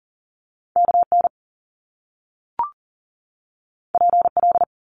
Morse code, heard as a keyed tone, about 40 wpm, 700 Hz. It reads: KN PL
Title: Top Two Letter Combinations in Words 40wpm